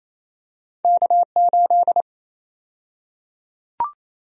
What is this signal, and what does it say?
Morse code, heard as a keyed tone, about 28 wpm, 700 Hz. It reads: K8
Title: USA Call Sign Prefixes - Repeated 2x - 28wpm